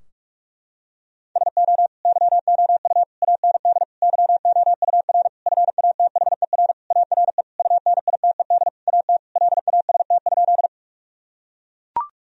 Morse code, heard as a keyed tone, about 43 wpm, 700 Hz. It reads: SO YOU AND YOUR FATHER ARE UNITED AT LAST?
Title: Sentences from Top 1000 Words - Review Part 4 - 45wpm